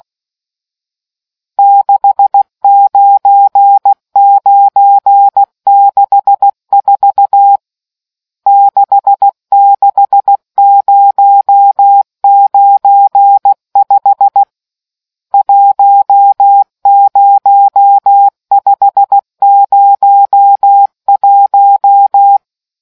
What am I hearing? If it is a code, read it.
Morse code, heard as a keyed tone, about 16 wpm, 800 Hz. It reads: 69964 66095 10501